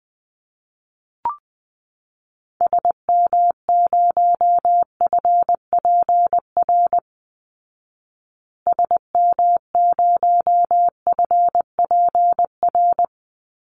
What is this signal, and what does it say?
Morse code, heard as a keyed tone, about 19 wpm, 700 Hz. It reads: SM0FPR SM0FPR
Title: Call Signs Repeated 3x - 20wpm